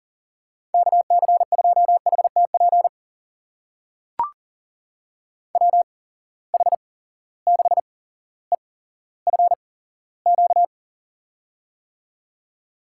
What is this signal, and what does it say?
Morse code, heard as a keyed tone, about 37 wpm, spaced as for 17 wpm, 700 Hz. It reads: KC2HTP WH6EFQ
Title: USA Call Signs - 6 Characters - 8x Character Spacing - 40wpm